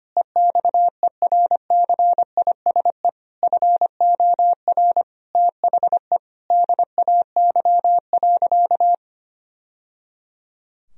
Morse code, heard as a keyed tone, about 25 wpm, 700 Hz. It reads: EXERCISE FOR THE DAY.
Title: Sentences from Top 1000 Words - Review Part 4 - 25wpm